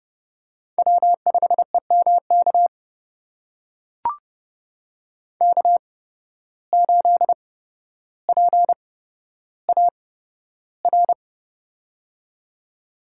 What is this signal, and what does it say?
Morse code, heard as a keyed tone, about 31 wpm, spaced as for 13 wpm, 700 Hz. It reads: W5EMK K8PAR
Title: USA Call Signs - 5 Characters - 8x Character Spacing - 30wpm